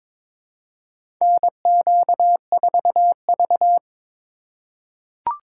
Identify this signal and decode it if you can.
Morse code, heard as a keyed tone, about 21 wpm, 690 Hz. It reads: NQ4V
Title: USA Call Signs - 4 Characters - 3x Character Spacing - 22wpm